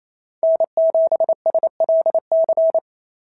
Morse code, heard as a keyed tone, about 28 wpm, 690 Hz. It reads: N7SLC
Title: SOTA QSO Practice 28wpm